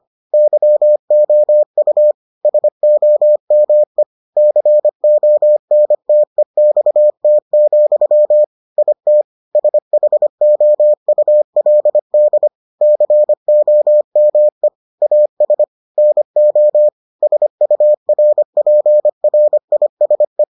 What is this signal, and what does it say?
Morse code, heard as a keyed tone, about 25 wpm, 600 Hz. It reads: YOU SOME CONTEXT, IT SHOULD COME AS NO SURPRISE